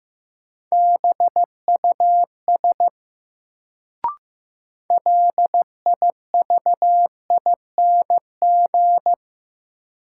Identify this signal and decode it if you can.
Morse code, heard as a keyed tone, about 15 wpm, 700 Hz. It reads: BUS LIVING
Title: Sentences from Top 1000 Words - Review Part 6 - New Word Drill - 15wpm